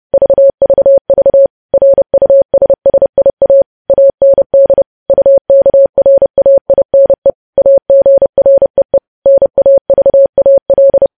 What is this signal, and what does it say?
Morse code, heard as a keyed tone, about 30 wpm, 550 Hz. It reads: VVV RUSSIA AND UKRAINE AGREE NAVAL
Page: news headlines in morse code at 30 wpm